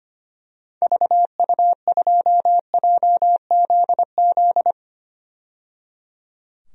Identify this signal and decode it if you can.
Morse code, heard as a keyed tone, about 25 wpm, 700 Hz. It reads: VU2JZZ
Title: DXCC-100 Call Signs - 6 Characters - 25wpm